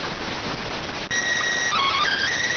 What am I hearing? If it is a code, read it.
SSTV (slow-scan television), PD 290.